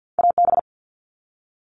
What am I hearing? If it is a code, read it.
Morse code, heard as a keyed tone, about 50 wpm, 700 Hz. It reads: AB
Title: Canadian Provinces and Territories - Repeated 3x - 50wpm